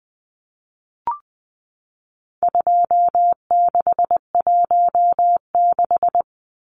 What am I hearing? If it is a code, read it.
Morse code, heard as a keyed tone, about 20 wpm, 700 Hz. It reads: 2616
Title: Numbers - 4-Digit 20wpm